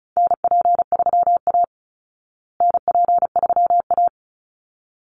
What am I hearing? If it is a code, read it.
Morse code, heard as a keyed tone, about 34 wpm, 700 Hz. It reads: NP3A NP3A
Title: Call Signs Repeated 3x - 35wpm